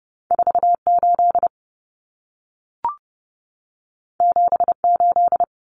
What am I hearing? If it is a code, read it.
Morse code, heard as a keyed tone, about 30 wpm, 700 Hz. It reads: 48 78
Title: Common Cut Numbers - 2-Digit 30wpm